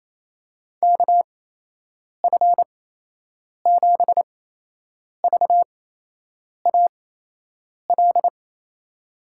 Morse code, heard as a keyed tone, about 28 wpm, spaced as for 8 wpm, 700 Hz. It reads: KF7VAL
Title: USA Call Signs - 6 Characters - 8x Character Spacing - 28wpm